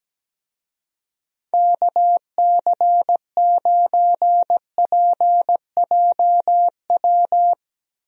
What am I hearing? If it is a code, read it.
Morse code, heard as a keyed tone, about 17 wpm, 700 Hz. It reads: KC9PJW
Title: USA Call Signs - 6 Characters - Repeated 3x - 17wpm